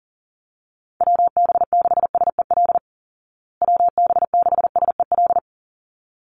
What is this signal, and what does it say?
Morse code, heard as a keyed tone, about 40 wpm, 700 Hz. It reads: WB6SEL WB6SEL
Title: Call Signs Repeated 3x - 40wpm